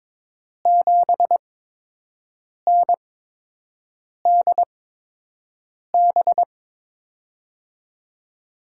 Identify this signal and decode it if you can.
Morse code, heard as a keyed tone, about 22 wpm, spaced as for 6 wpm, 700 Hz. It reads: 7NDB
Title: USA Call Signs - 5 Characters - 8x Character Spacing - 22wpm